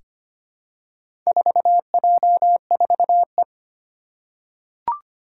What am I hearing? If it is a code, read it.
Morse code, heard as a keyed tone, about 25 wpm, 700 Hz. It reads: 4J4E
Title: Four Challenging Letter-Number Combinations 25wpm